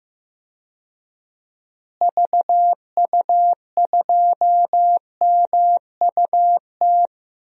Morse code, heard as a keyed tone, about 15 wpm, 700 Hz. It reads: VU2MUT
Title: DXCC-100 Call Signs - 6 Characters - Repeated 2x - 15wpm